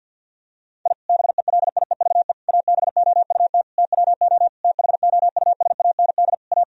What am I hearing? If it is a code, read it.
Morse code, heard as a keyed tone, about 50 wpm, 700 Hz. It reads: I BELIEVE ABOUT TWO THOUSAND A
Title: Sentences from Top 1000 Words - Review Part 3 - 50wpm